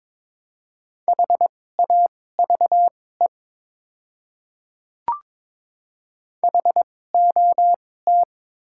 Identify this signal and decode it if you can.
Morse code, heard as a keyed tone, about 22 wpm, spaced as for 16 wpm, 690 Hz. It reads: HAVE HOT
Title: 2x Character Spacing - Top 100 Words - 22wpm